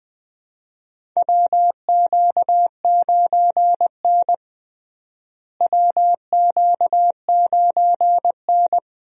Morse code, heard as a keyed tone, about 21 wpm, 700 Hz. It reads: WQ9N WQ9N
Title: USA Call Signs - 4 Characters - Repeated 2x - 20wpm